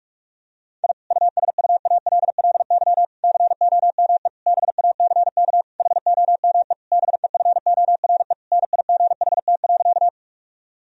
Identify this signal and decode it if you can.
Morse code, heard as a keyed tone, about 43 wpm, 700 Hz. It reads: I USUALLY COME BACK HOME BEFORE NIGHT.
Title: Sentences from Top 1000 Words - Review Part 3 - 45wpm